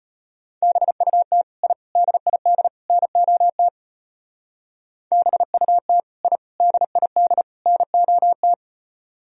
Morse code, heard as a keyed tone, about 34 wpm, 700 Hz. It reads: BUT I DID NOT BUT I DID NOT